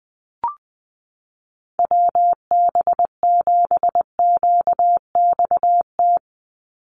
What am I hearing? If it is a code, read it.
Morse code, heard as a keyed tone, about 20 wpm, 700 Hz. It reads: WB7QXT
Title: USA Call Signs 20wpm